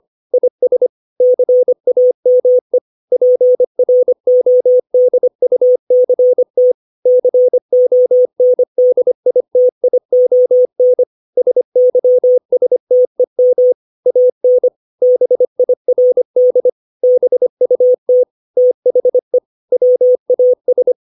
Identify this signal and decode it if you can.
Morse code, heard as a keyed tone, about 25 wpm, 500 Hz. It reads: IS CAME PRODUCT CONDITION SYSTEM AN BIRD BUT THE WAS
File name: 632 words / 632 Words 25WPM